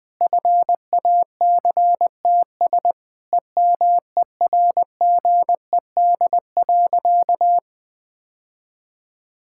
Morse code, heard as a keyed tone, about 20 wpm, 700 Hz. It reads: FACTS EMERGED.
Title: Sentences with 5-Word Phrases - 20wpm